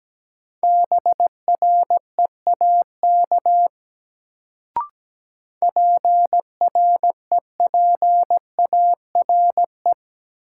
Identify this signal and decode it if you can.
Morse code, heard as a keyed tone, about 17 wpm, 700 Hz. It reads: BREAK PREPARE
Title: Sentences from Top 900 Words - New Word Drill - 17wpm